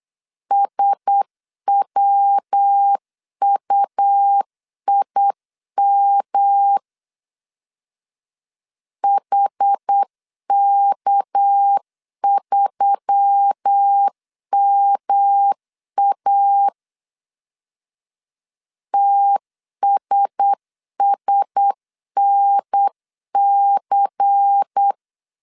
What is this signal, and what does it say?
Morse code, heard as a keyed tone, about 8 wpm, 800 Hz. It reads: SWUIM HK3MA TSSNC
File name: CW / DET / 1 6 1500C S C 8wpm 5219